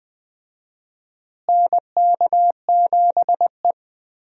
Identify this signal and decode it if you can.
Morse code, heard as a keyed tone, about 21 wpm, 700 Hz. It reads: NK7E
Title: USA Call Signs - 4 Characters - Repeated 2x - 20wpm